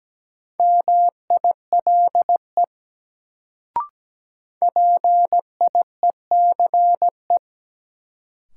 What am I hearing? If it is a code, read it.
Morse code, heard as a keyed tone, about 17 wpm, 700 Hz. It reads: MILE PIECE
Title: Sentences from Top 400 Words - New Word Drill - 17wpm